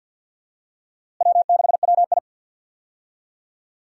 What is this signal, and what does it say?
Morse code, heard as a keyed tone, about 52 wpm, 700 Hz. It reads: W6WI
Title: USA Call Signs - 4 Characters - Repeated 2x - 50wpm